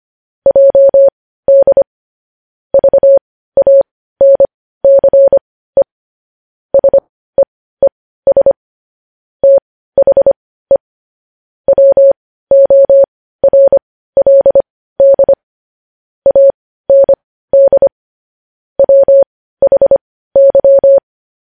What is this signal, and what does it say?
Morse code, heard as a keyed tone, about 25 wpm, spaced as for 15 wpm, 550 Hz. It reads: JD VANCE SEES THE WORLD AND WHY
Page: News Headlines in Morse Code at 15 WPM